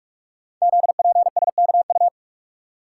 Morse code, heard as a keyed tone, about 44 wpm, 700 Hz. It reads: ZPSKU
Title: Code Groups - 5 Letter:Numbers - 45wpm